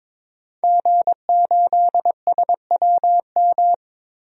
Morse code, heard as a keyed tone, about 22 wpm, 700 Hz. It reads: G8SWM
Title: DXCC-100 Call Signs - 5 Characters - Repeated 3x - 22wpm